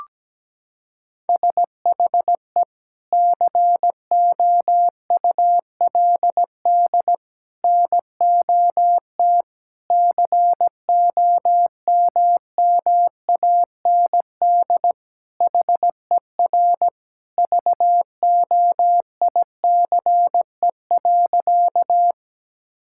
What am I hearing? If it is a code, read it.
Morse code, heard as a keyed tone, about 17 wpm, 700 Hz. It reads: SHE COULD NOT COMMAND HER VOICE.
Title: Sentences from Top 1600 Words 17wpm